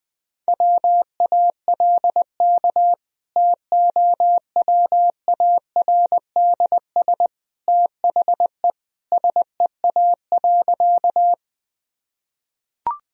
Morse code, heard as a keyed tone, about 21 wpm, 700 Hz. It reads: WALK TOWARDS THE SEA.
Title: Sentences from Top 1000 Words - Review Part 2 - 20wpm